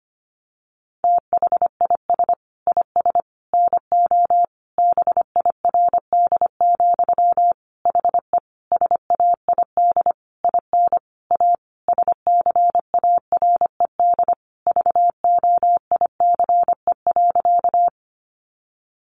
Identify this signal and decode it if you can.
Morse code, heard as a keyed tone, about 25 wpm, 700 Hz. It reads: THIS IS NO BIRD, HE SAID IN A SCARED VOICE.
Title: Sentences from Top 700 Words 25wpm